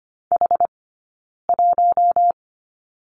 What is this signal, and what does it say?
Morse code, heard as a keyed tone, about 25 wpm, 700 Hz. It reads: H 1
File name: Lesson 17d - Letter u - Two Characters 25wpm